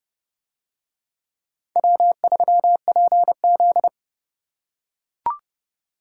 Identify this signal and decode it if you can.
Morse code, heard as a keyed tone, about 29 wpm, 700 Hz. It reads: W3PZ